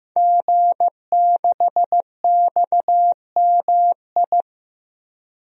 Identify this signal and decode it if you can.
Morse code, heard as a keyed tone, about 15 wpm, 700 Hz. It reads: G6XMI
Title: DXCC-100 Call Signs - 5 Characters - Repeated 2x - 15wpm